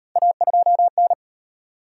Morse code, heard as a keyed tone, about 37 wpm, 700 Hz. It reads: A2N